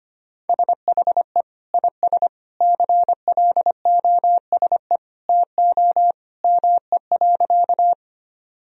Morse code, heard as a keyed tone, about 25 wpm, 700 Hz. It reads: SHE IS CLOSE TO ME.
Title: Sentences from Top 1000 Words - Review Part 1 - 25wpm